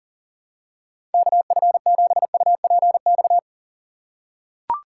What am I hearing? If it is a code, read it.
Morse code, heard as a keyed tone, about 37 wpm, 700 Hz. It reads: KF7UPX
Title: USA Call Signs - 6 Characters - 6x Character Spacing - 40wpm